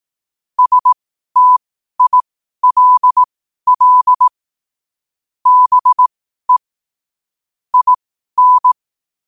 Morse code, heard as a keyed tone, about 18 wpm, spaced as for 12 wpm, 1000 Hz. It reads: STILL BE IN
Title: 12 WPM morse code quotes for Thu, 29 Feb 2024